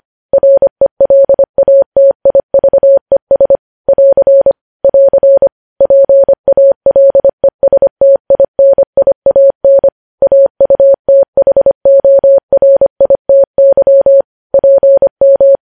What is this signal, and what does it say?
Morse code, heard as a keyed tone, about 25 wpm, 550 Hz. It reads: RELATIVES <AR> <AR> PALESTINIAN AUTHORITY PM
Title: News Headlines at 25